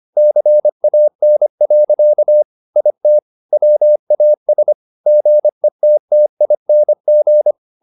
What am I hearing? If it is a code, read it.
Morse code, heard as a keyed tone, about 25 wpm, 600 Hz. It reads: CAN. IT WAS GETTING
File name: the old man and the sea part 1 25wpm